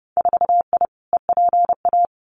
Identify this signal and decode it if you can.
Morse code, heard as a keyed tone, about 30 wpm, 700 Hz. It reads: 4I EPA
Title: Winter Field Day Exchanges Encore 30wpm